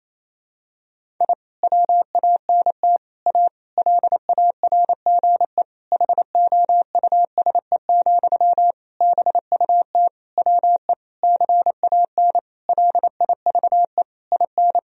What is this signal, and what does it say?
Morse code, heard as a keyed tone, about 28 wpm, 700 Hz. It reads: I WANT A LARGE HOUSE, BUT WE CAN LIVE IN
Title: Sentences from Top 1000 Words - Review Part 1 - 28wpm